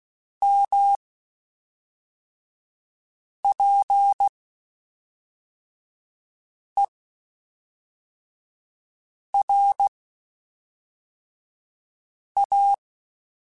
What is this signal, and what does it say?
Morse code, heard as a keyed tone, about 16 wpm, 800 Hz. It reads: M P E R A
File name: Spaced Characters / Spaced Characters-mp3s / Weather5 22 sp @5